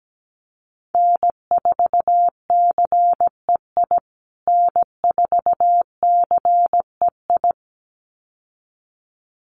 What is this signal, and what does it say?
Morse code, heard as a keyed tone, about 17 wpm, 700 Hz. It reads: N4CEI N4CEI
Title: USA Call Signs Repeated 2x 17wpm